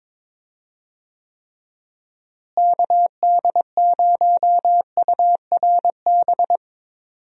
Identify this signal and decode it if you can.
Morse code, heard as a keyed tone, about 22 wpm, 700 Hz. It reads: KD0URB
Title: USA Call Signs - 6 Characters - Repeated 3x - 22wpm